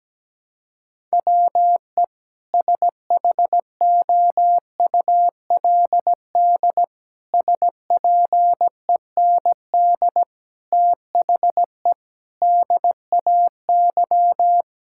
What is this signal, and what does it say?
Morse code, heard as a keyed tone, about 17 wpm, 700 Hz. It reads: WE SHOULD SPEND THE DAY
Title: Sentences from Top 1000 Words - Review Part 5 - 17wpm